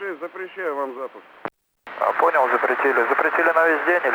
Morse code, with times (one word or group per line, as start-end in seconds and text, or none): none